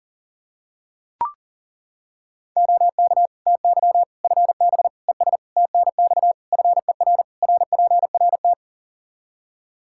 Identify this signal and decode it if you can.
Morse code, heard as a keyed tone, about 40 wpm, 700 Hz. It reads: OK TY FB ES TNX FER RPRT
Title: K1USN SST QSO Practice 40wpm